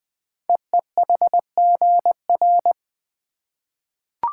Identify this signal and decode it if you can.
Morse code, heard as a keyed tone, about 20 wpm, 700 Hz. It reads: EEHGR
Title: Code Groups - 5 Letters - 20wpm